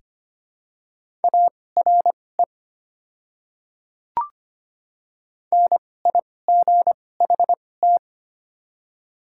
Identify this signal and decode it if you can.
Morse code, heard as a keyed tone, about 25 wpm, spaced as for 18 wpm, 700 Hz. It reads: ARE NIGHT
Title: 2x Character Spacing - Top 500 Words - 25wpm